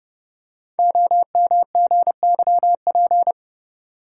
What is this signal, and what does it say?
Morse code, heard as a keyed tone, about 30 wpm, 690 Hz. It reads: OMGYP
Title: Code Groups - 5 Letters - 30wpm